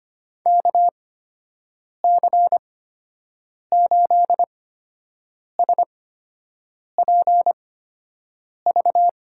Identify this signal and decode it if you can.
Morse code, heard as a keyed tone, about 25 wpm, spaced as for 7 wpm, 700 Hz. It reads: KC8SPV
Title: USA Call Signs - 6 Characters - 8x Character Spacing - 25wpm